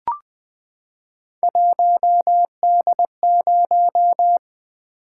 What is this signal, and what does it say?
Morse code, heard as a keyed tone, about 20 wpm, 700 Hz. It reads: 1D0
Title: Three Letter-Number Combinations - 20wpm